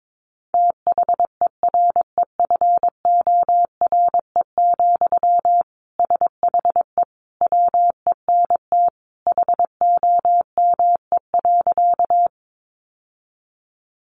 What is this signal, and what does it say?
Morse code, heard as a keyed tone, about 22 wpm, 700 Hz. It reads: THEREFORE, SHE WENT HOME.
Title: Sentences from Top 800 Words 22wpm